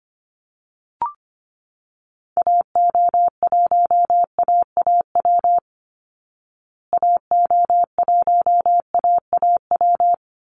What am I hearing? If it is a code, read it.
Morse code, heard as a keyed tone, about 25 wpm, 700 Hz. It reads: AO1AAW AO1AAW
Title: Call Signs Repeated 3x - 25wpm